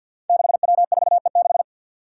Morse code, heard as a keyed tone, about 47 wpm, 700 Hz. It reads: 6W4E6
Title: Five Challenging Letter-Number Combinations 50wpm